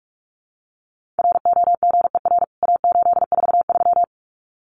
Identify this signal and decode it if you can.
Morse code, heard as a keyed tone, about 44 wpm, 700 Hz. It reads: ROGER A843
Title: CWT Exchange 45wpm